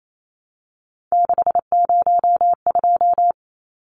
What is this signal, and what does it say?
Morse code, heard as a keyed tone, about 28 wpm, 700 Hz. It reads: 602
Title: Numbers - 3-Digit 28wpm